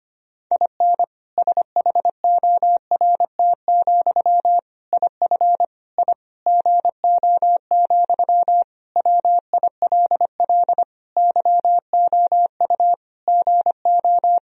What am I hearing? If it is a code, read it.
Morse code, heard as a keyed tone, about 25 wpm, 700 Hz. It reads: IN SHORT, IF I GO, WILL YOU GO
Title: Sentences from Top 1000 Words - Review Part 2 - 25wpm